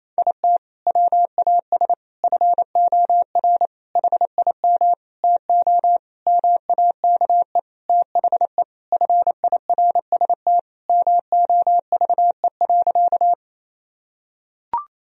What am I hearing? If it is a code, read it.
Morse code, heard as a keyed tone, about 28 wpm, 700 Hz. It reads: IT WAS FOR HIM TO MAKE THE FIRST MOVE.
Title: Sentences from Top 1000 Words - Review Part 1 - 28wpm